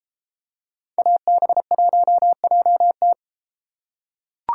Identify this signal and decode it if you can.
Morse code, heard as a keyed tone, about 32 wpm, 700 Hz. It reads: AB1JT